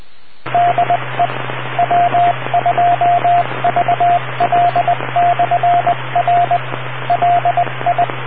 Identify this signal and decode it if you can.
Morse code, heard as a keyed tone, about 20 wpm, 690 Hz, with radio noise. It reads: DE W2VL/R LI